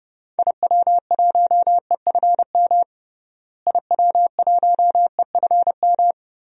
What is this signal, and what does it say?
Morse code, heard as a keyed tone, about 29 wpm, 700 Hz. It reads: IW1EFM IW1EFM
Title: DXCC-100 Call Signs - 6 Characters - Repeated 2x - 30wpm